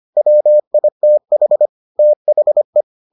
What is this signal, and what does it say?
Morse code, heard as a keyed tone, about 25 wpm, 600 Hz. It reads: WITH THE